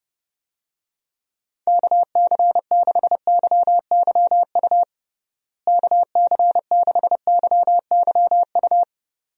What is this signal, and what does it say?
Morse code, heard as a keyed tone, about 30 wpm, 700 Hz. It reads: KC6YYU KC6YYU
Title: USA Call Signs - 6 Characters - Repeated 2x - 30wpm